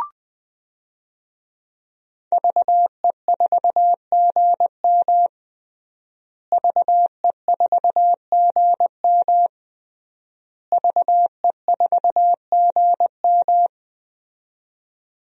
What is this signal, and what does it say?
Morse code, heard as a keyed tone, about 20 wpm, 700 Hz. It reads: VE4GM VE4GM VE4GM
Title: DXCC-100 Call Signs - 5 Characters - Repeated 3x - 20wpm